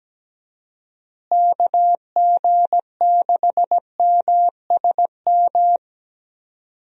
Morse code, heard as a keyed tone, about 17 wpm, 700 Hz. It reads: KG6MSM
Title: USA Call Signs - 6 Characters - Repeated 3x - 17wpm